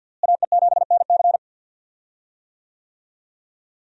Morse code, heard as a keyed tone, about 47 wpm, 700 Hz. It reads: AE7NC
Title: USA Call Signs - 5 Characters - Repeated 2x - 50wpm